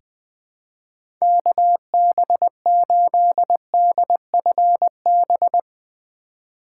Morse code, heard as a keyed tone, about 21 wpm, 700 Hz. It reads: KB8DFB
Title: USA Call Signs - 2x Character Spacing - 20wpm